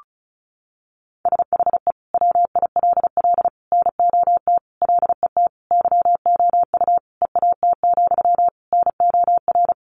Morse code, heard as a keyed tone, about 34 wpm, 700 Hz. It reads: SHE WILL NOT LET YOU EAT, NOR
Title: Sentences from Top 1500 Words 35wpm